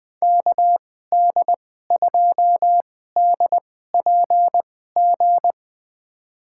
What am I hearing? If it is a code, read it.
Morse code, heard as a keyed tone, about 20 wpm, spaced as for 15 wpm, 700 Hz. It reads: KD2DPG
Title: USA Call Signs - 6 Characters - 2x Character Spacing - 20wpm